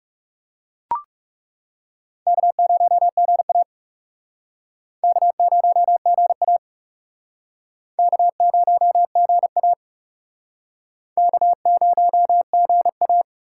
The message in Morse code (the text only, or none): K0GA K0GA K0GA K0GA